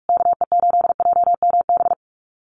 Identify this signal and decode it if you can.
Morse code, heard as a keyed tone, about 44 wpm, 700 Hz. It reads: KE8JMB
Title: USA Call Signs Repeated 3x 45wpm